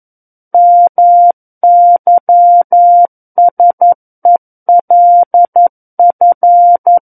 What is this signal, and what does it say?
Morse code, heard as a keyed tone, about 11 wpm, 700 Hz. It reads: MYSELF
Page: SKCC - War of the Worlds - 02-Chapter 2 - 11 WPM